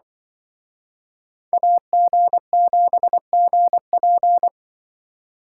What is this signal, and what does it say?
Morse code, heard as a keyed tone, about 24 wpm, 700 Hz. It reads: AG7GP
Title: Speed Racing - USA Call Signs 20wpm